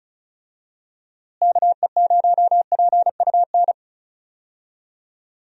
Morse code, heard as a keyed tone, about 36 wpm, 700 Hz. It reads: KE0PUN
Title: USA Call Signs - 6 Characters - Repeated 2x - 35wpm